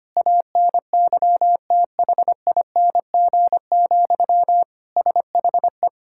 Morse code, heard as a keyed tone, about 25 wpm, 700 Hz. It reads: ANYTHING, SHE
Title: Sentences from Top 1000 Words - Review Part 6 - 25wpm